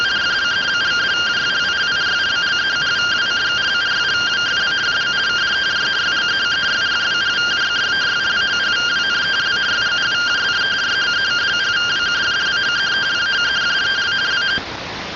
Two-tone teleprinter signal, about 170 Hz shift.